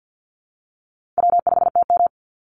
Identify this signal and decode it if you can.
Morse code, heard as a keyed tone, about 50 wpm, 700 Hz. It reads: W5TM
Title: Easy Call Signs Repeated 2x - 50wpm